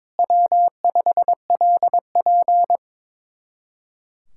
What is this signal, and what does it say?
Morse code, heard as a keyed tone, about 23 wpm, 700 Hz. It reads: W5LP
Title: Speed-Racing Common QSO Phrases - 22wpm